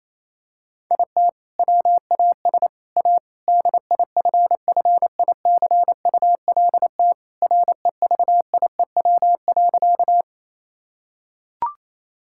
Morse code, heard as a keyed tone, about 28 wpm, 700 Hz. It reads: IT WAS A DIFFICULT REVIEW.
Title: Sentences from Top 1000 Words - Review Part 6 - 28wpm